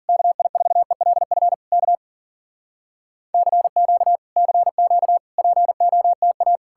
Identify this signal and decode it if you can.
Morse code, heard as a keyed tone, about 60 wpm, spaced as for 51 wpm, 700 Hz. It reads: KI4ELF K CQ CQ POTA